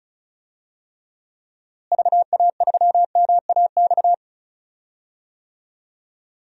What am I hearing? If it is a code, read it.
Morse code, heard as a keyed tone, about 34 wpm, 700 Hz. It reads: VA3MAX
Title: DXCC-100 Call Signs - 6 Characters - Repeated 2x - 35wpm